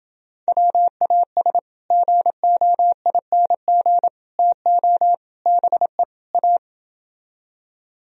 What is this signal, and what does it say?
Morse code, heard as a keyed tone, about 29 wpm, 700 Hz. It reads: WAS GOING TO BE A